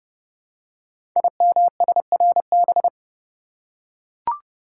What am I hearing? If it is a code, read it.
Morse code, heard as a keyed tone, about 28 wpm, 700 Hz. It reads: IMSRB
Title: Code Groups - 5 Letters - 30wpm